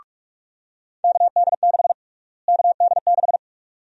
Morse code, heard as a keyed tone, about 46 wpm, 700 Hz. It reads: KD6 KD6
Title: USA Call Sign Prefixes - Repeated 2x - 45wpm